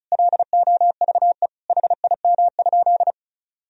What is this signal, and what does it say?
Morse code, heard as a keyed tone, about 36 wpm, 700 Hz. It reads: LOVE HIM?
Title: Sentences from Top 400 Words and Abbreviations 35wpm